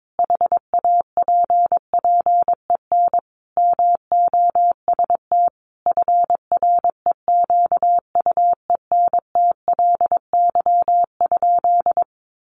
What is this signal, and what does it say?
Morse code, heard as a keyed tone, about 22 wpm, 700 Hz. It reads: HAPPEN MOST FREQUENTLY?
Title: Sentences from Top 1300 Words 22wpm